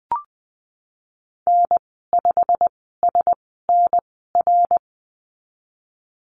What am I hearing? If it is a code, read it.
Morse code, heard as a keyed tone, about 20 wpm, spaced as for 15 wpm, 700 Hz. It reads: N5SNR
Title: 122 Lesson 20g - Number 3 - Call Signs 20wpm